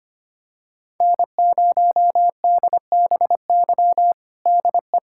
Morse code, heard as a keyed tone, about 25 wpm, 700 Hz. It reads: N0DBY DE
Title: USA Callsign DE Callsign 25wpm